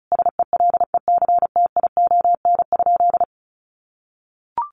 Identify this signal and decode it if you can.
Morse code, heard as a keyed tone, about 34 wpm, 700 Hz. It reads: SELECTION?
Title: Sentences from Top 1400 Words 35wpm